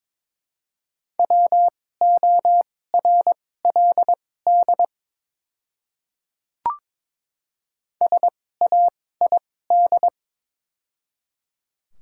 Morse code, heard as a keyed tone, about 22 wpm, spaced as for 16 wpm, 700 Hz. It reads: WORLD SAID
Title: 2x Character Spacing - Top 300 Words - 22wpm